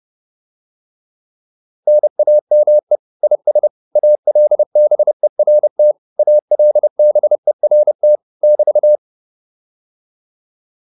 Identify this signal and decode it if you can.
Morse code, heard as a keyed tone, about 30 wpm, 600 Hz. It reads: NAME IS ALBERT ALBERT <BT>
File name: QSO109 30